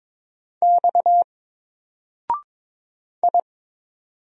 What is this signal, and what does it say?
Morse code, heard as a keyed tone, about 22 wpm, 700 Hz. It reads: X I
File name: Single-Letter-Number-22wpm